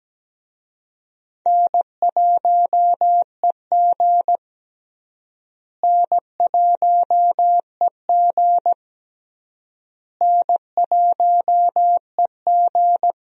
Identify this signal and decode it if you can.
Morse code, heard as a keyed tone, about 17 wpm, 700 Hz. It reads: N1EG N1EG N1EG
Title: USA Call Signs - 4 Characters - Repeated 3x - 17wpm